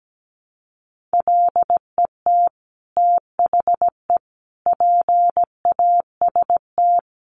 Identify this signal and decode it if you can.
Morse code, heard as a keyed tone, about 17 wpm, 700 Hz. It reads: LET THE PAST
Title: Sentences from Top 1200 Words 17wpm